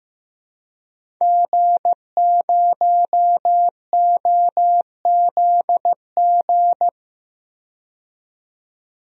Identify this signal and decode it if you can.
Morse code, heard as a keyed tone, about 16 wpm, 700 Hz. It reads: G0OZG
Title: DXCC-100 Call Signs - 5 Characters - 15wpm